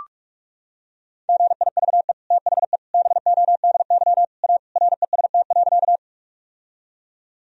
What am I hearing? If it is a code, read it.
Morse code, heard as a keyed tone, about 46 wpm, 700 Hz. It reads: GIVE THE BODY A REST.
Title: Sentences from Top 1000 Words - Review Part 2 - 45wpm